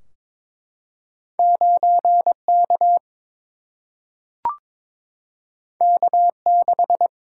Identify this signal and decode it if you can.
Morse code, heard as a keyed tone, about 22 wpm, 700 Hz. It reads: 9K K6
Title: Single-Letter Single-Number 22wpm